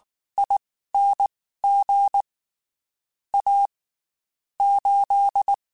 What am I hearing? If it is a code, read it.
Morse code, heard as a keyed tone, about 19 wpm, spaced as for 14 wpm, 800 Hz. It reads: ING A 8
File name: QSOs / QSOs-mp3s / WD0FBT 1@18